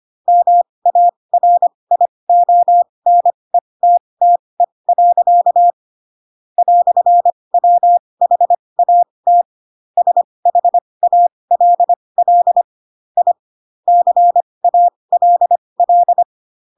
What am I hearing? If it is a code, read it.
Morse code, heard as a keyed tone, about 25 wpm, spaced as for 20 wpm, 700 Hz. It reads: MARIONETTE. "WHAT SHALL I CALL
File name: Pinocchio / Pinocchio ch3 15 25 ik1hge